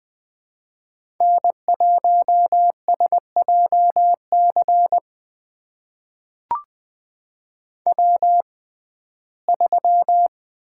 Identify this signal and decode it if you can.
Morse code, heard as a keyed tone, about 20 wpm, spaced as for 15 wpm, 700 Hz. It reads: N1SJC W3
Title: USA Call Signs - 5 Characters - 6x Character Spacing - 20wpm